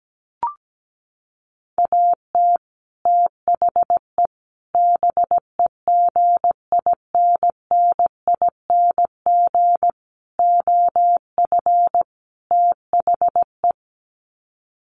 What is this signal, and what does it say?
Morse code, heard as a keyed tone, about 17 wpm, 700 Hz. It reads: AT THE BEGINNING OF THE
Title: Sets of 5 Words 17wpm